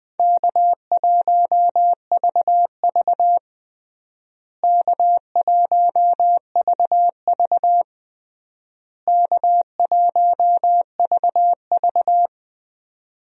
Morse code, heard as a keyed tone, about 20 wpm, 700 Hz. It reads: K1VV K1VV K1VV
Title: USA Call Signs - 4 Characters - Repeated 3x - 20wpm